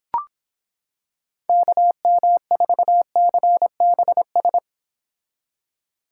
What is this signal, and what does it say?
Morse code, heard as a keyed tone, about 26 wpm, 700 Hz. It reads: KM4CBS